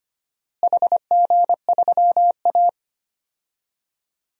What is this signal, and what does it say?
Morse code, heard as a keyed tone, about 25 wpm, 700 Hz. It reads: HG3A
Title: DXCC-100 Call Signs - 4 Characters - Repeated 2x - 25wpm